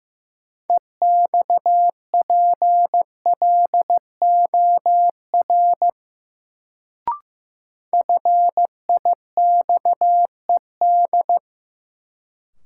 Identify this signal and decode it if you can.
Morse code, heard as a keyed tone, about 16 wpm, 700 Hz. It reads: EXPLOR FIXED
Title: Sentences from Top 1100 Words - New Word Drill - 15wpm